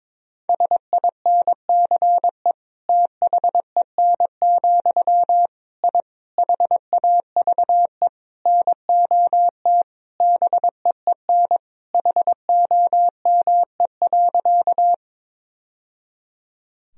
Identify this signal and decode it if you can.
Morse code, heard as a keyed tone, about 22 wpm, 700 Hz. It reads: SINCE THEN, I HAVE NOT BEEN HOME.
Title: Sentences from Top 1000 Words - Review Part 1 - 22wpm